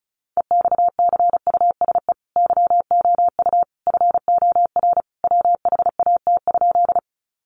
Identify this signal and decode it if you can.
Morse code, heard as a keyed tone, about 34 wpm, 700 Hz. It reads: EXCUSE YOU FOR WHAT?
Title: Sentences from Top 700 Words 35wpm